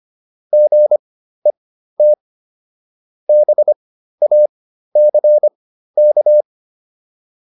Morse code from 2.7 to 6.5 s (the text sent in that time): BACK